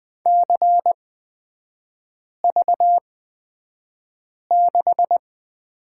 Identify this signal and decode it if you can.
Morse code, heard as a keyed tone, about 20 wpm, spaced as for 5 wpm, 700 Hz. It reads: CV6
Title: 243 Lesson 39e - Letter z - Three Characters 20wpm